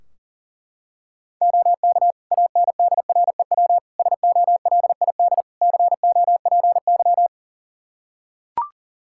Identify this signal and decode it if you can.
Morse code, heard as a keyed tone, about 40 wpm, 700 Hz. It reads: OK ANDREW SOLID COPY
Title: Practice QSO - Exchange 3 - More Info - 40wpm